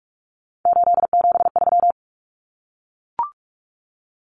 Morse code, heard as a keyed tone, about 44 wpm, 700 Hz. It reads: 873
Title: Numbers - 3-Digit 45wpm